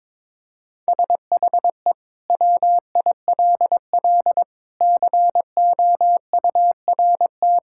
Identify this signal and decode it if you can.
Morse code, heard as a keyed tone, about 22 wpm, 700 Hz. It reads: SHE WILL COURT
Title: Sentences from Top 1600 Words 22wpm